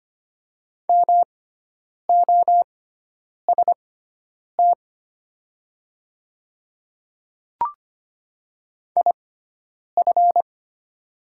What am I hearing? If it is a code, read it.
Morse code, heard as a keyed tone, about 25 wpm, spaced as for 9 wpm, 700 Hz. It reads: MOST IF
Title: 6x Character Spacing - Top 100 Words - 25wpm